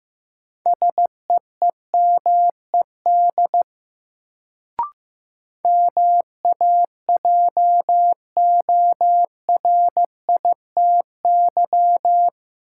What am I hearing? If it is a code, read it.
Morse code, heard as a keyed tone, about 15 wpm, 700 Hz. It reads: SEEMED MAJORITY
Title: Sets of 3 Words - New Word Drill - 15wpm